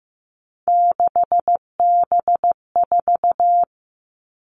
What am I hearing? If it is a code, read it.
Morse code, heard as a keyed tone, about 15 wpm, 700 Hz. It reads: 6B4
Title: All Cut Numbers - 3-Digit 15wpm